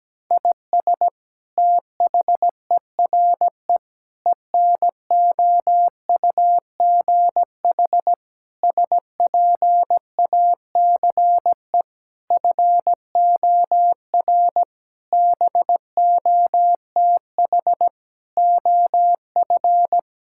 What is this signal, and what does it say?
Morse code, heard as a keyed tone, about 17 wpm, 700 Hz. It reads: IS THERE ENOUGH SPACE FOR BOTH OF
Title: Sentences from Top 1000 Words - Review Part 5 - 17wpm